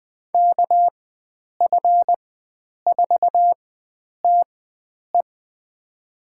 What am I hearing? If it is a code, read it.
Morse code, heard as a keyed tone, about 20 wpm, spaced as for 9 wpm, 700 Hz. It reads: KF4TE
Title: USA Call Signs - 5 Characters - 4x Character Spacing - 20wpm